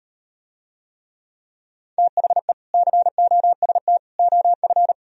Morse code, heard as a keyed tone, about 38 wpm, 700 Hz. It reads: THE COST OF